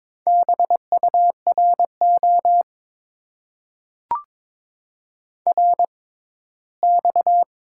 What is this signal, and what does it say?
Morse code, heard as a keyed tone, about 22 wpm, spaced as for 15 wpm, 700 Hz. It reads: BURO RX
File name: Top 90 QSO Elements - 6x Character Spacing - 22wpm